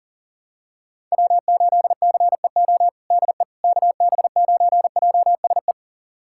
Morse code, heard as a keyed tone, about 40 wpm, 700 Hz. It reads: W8CEO DE KB9JSE
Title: USA Callsign DE Callsign 40wpm